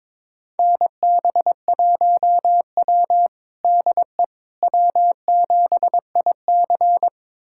Morse code, heard as a keyed tone, about 22 wpm, 700 Hz. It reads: NB1W DE W7IC
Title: USA Callsign DE Callsign - 4 Characters - 22wpm